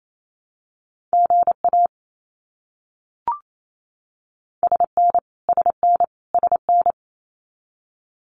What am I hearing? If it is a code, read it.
Morse code, heard as a keyed tone, about 28 wpm, 700 Hz. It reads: GA SN SN SN
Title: Abbreviations Repeated 3x - 28wpm